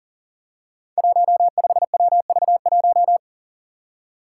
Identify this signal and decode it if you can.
Morse code, heard as a keyed tone, about 39 wpm, 700 Hz. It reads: 15WV1
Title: Five Challenging Letter-Number Combinations 40wpm